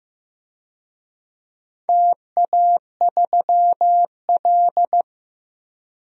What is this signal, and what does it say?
Morse code, heard as a keyed tone, about 15 wpm, 700 Hz. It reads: TA3L
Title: DXCC-100 Call Signs - 4 Characters - 15wpm